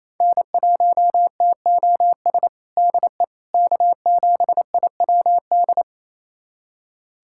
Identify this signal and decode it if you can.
Morse code, heard as a keyed tone, about 28 wpm, 700 Hz. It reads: N1TOS DE K7IWD
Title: USA Callsign DE Callsign - 5 Characters - 28wpm